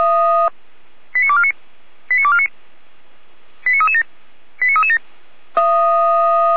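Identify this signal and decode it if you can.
Five-tone selective call (ZVEI), heard as quick runs of five tones, tones of about 70 ms. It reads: 89249 89249 89337 89337
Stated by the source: ZVEI five-tone selective call received by SDR, from feuerwehr-orlamuende (Orlamünde fire brigade) audio archive.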